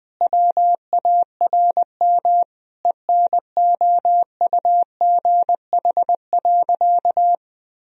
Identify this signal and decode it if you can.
Morse code, heard as a keyed tone, about 21 wpm, 700 Hz. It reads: WARM ENOUGH.
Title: Sentences from Top 1000 Words - Review Part 2 - 20wpm